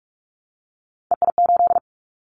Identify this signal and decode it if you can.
Morse code, heard as a keyed tone, about 44 wpm, 700 Hz. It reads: EI8
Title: Callsign Prefixes for Top 50 DXCC - 45wpm